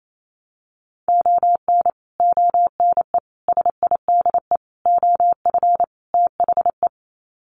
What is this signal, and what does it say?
Morse code, heard as a keyed tone, about 28 wpm, 700 Hz. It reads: ON ONE SIDE OF THE
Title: Sets of 5 Words 28wpm